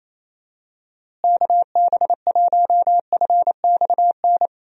Morse code, heard as a keyed tone, about 28 wpm, 700 Hz. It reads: KB1FXN
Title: USA Call Signs - 6 Characters - Repeated 3x - 28wpm